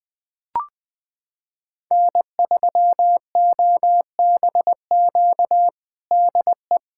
Morse code, heard as a keyed tone, about 20 wpm, 700 Hz. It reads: N3OBQ DE
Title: USA Callsign DE Callsign - 5 Characters - 20wpm